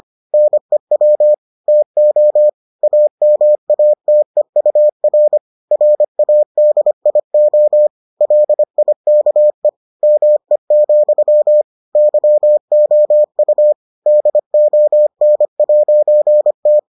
Morse code, heard as a keyed tone, about 25 wpm, 600 Hz. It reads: NEW TO AMATEUR RADIO LIKE ME, YOU DON'T